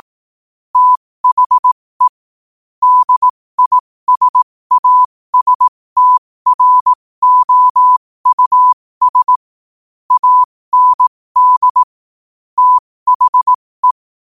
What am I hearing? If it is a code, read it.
Morse code, heard as a keyed tone, about 18 wpm, spaced as for 15 wpm, 1000 Hz. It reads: THE DISASTROUS AND THE
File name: qotdfiles / 15wpm